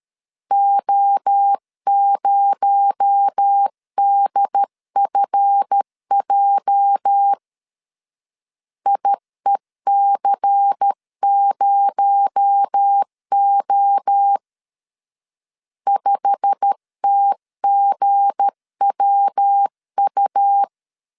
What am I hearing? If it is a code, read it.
Morse code, heard as a keyed tone, about 13 wpm, 800 Hz. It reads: O0DFJ IEC0O 5TGWU
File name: LEZIONI / 8 YJ45 / 8 YJ45 12wpm 2151